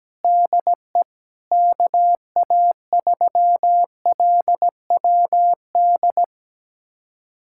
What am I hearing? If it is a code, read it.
Morse code, heard as a keyed tone, about 17 wpm, 700 Hz. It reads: DE KA3LWD
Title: USA Callsign DE Callsign 17wpm